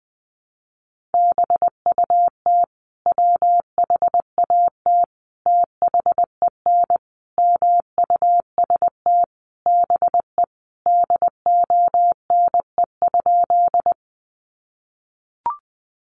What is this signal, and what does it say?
Morse code, heard as a keyed tone, about 20 wpm, 700 Hz. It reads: BUT WHAT THEN MUST BE DONE?
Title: Sentences from Top 500 Words 20wpm